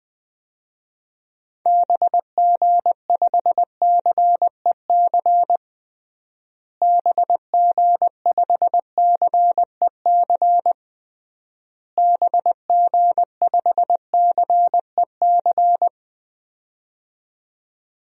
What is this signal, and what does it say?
Morse code, heard as a keyed tone, about 20 wpm, 700 Hz. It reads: BG5CEC BG5CEC BG5CEC
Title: DXCC-100 Call Signs - 6 Characters - Repeated 3x - 20wpm